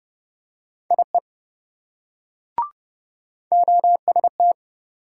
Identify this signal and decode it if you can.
Morse code, heard as a keyed tone, about 27 wpm, 700 Hz. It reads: IE OST